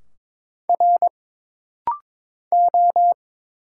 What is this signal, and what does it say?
Morse code, heard as a keyed tone, about 22 wpm, 700 Hz. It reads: R O
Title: Single Letters - Rapid-Fire 22wpm